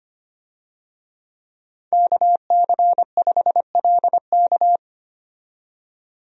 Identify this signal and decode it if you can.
Morse code, heard as a keyed tone, about 25 wpm, 700 Hz. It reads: KC5LK
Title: USA Call Signs - 5 Characters - 25wpm